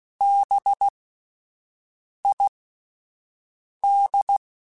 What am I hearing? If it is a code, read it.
Morse code, heard as a keyed tone, about 16 wpm, spaced as for 5 wpm, 800 Hz. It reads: BID
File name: Alphabetical / Alphabetical mp3s / Bs @5